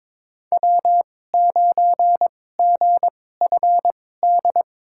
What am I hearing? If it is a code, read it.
Morse code, heard as a keyed tone, about 22 wpm, spaced as for 16 wpm, 700 Hz. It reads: W9GFD
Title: USA Call Signs - 5 Characters - 2x Character Spacing - 22wpm